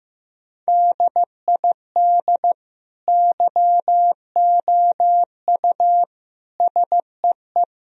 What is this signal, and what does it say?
Morse code, heard as a keyed tone, about 15 wpm, 700 Hz. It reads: DID YOU SEE
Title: Sentences from Top 1000 Words - Review Part 1 - 15wpm